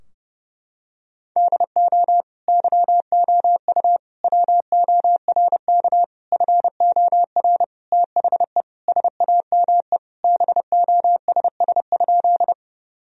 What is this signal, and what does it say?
Morse code, heard as a keyed tone, about 30 wpm, 700 Hz. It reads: DO YOU WORK FOR THE SAME BOSS?
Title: Sentences from Top 1000 Words - Review Part 4 - 30wpm